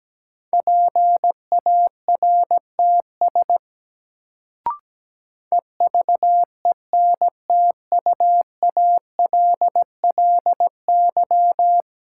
Morse code, heard as a keyed tone, about 17 wpm, 700 Hz. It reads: PARTS EVENTUALLY
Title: Sentences from Top 1000 Words - Review Part 6 - New Word Drill - 17wpm